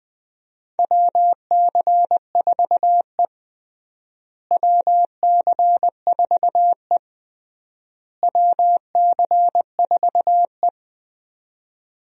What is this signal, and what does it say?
Morse code, heard as a keyed tone, about 20 wpm, 700 Hz. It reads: WC4E WC4E WC4E
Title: USA Call Signs - 4 Characters - Repeated 3x - 20wpm